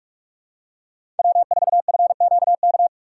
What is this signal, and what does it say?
Morse code, heard as a keyed tone, about 46 wpm, 700 Hz. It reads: W4FQK
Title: USA Call Signs - 5 Characters - 3x Character Spacing - 45wpm